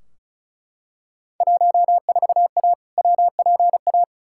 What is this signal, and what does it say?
Morse code, heard as a keyed tone, about 36 wpm, 700 Hz. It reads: 14A WPA
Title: Field Day Exchanges Encore - 35wpm